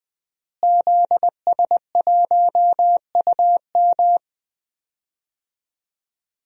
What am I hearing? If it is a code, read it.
Morse code, heard as a keyed tone, about 21 wpm, 700 Hz. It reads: ZS1UM
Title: DXCC-100 Call Signs - 5 Characters - Repeated 3x - 20wpm